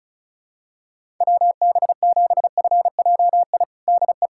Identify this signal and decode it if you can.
Morse code, heard as a keyed tone, about 36 wpm, 700 Hz. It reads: WB7FJI DE
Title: USA Callsign DE Callsign - 6 Characters - 35wpm